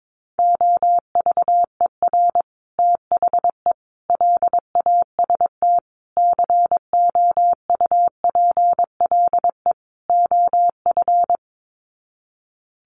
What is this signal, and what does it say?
Morse code, heard as a keyed tone, about 22 wpm, 700 Hz. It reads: OVER THE LAST COUPLE OF
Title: Sets of 5 Words Encore 22wpm